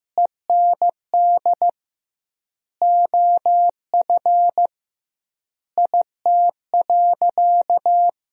Morse code, heard as a keyed tone, about 16 wpm, 700 Hz. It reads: END OF IT.
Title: 2x Word-Spacing - Sentences from Top 200 Words - 15wpm